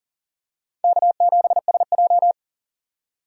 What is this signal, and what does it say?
Morse code, heard as a keyed tone, about 37 wpm, 700 Hz. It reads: K7SJ
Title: USA Call Signs - 4 Characters - 40wpm